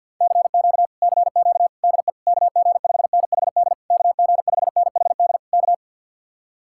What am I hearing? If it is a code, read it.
Morse code, heard as a keyed tone, about 61 wpm, 700 Hz. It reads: CQ CQ DE KG5NHD KG5NHD K
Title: Speed-Racing Common QSO Phrases - 50wpm